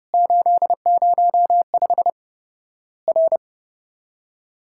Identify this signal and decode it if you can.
Morse code, heard as a keyed tone, about 30 wpm, 700 Hz. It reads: 805 R
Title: Worked All Europe DX Contest - QTC Exchange - 30wpm